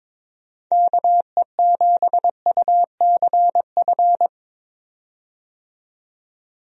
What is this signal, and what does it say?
Morse code, heard as a keyed tone, about 22 wpm, 700 Hz. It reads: KE7UCF
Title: USA Call Signs - 6 Characters - 22wpm